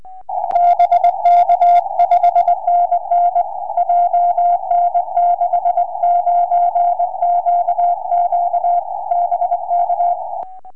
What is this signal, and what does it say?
Morse code, heard as a keyed tone, about 21 wpm, 690 Hz. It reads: BK 5NN JN69QQ BK
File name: ok7gu 4m